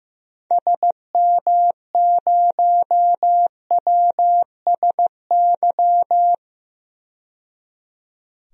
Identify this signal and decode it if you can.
Morse code, heard as a keyed tone, about 15 wpm, 700 Hz. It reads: SM0WSY
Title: DXCC-100 Call Signs - 6 Characters - 15wpm